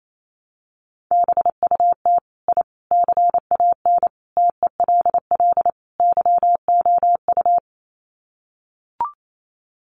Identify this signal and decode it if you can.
Morse code, heard as a keyed tone, about 28 wpm, 700 Hz. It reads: BUT I CAN TELL YOU
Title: Sets of 5 Words Encore 28wpm